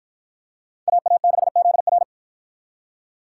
Morse code, heard as a keyed tone, about 44 wpm, 700 Hz. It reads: AA6ZR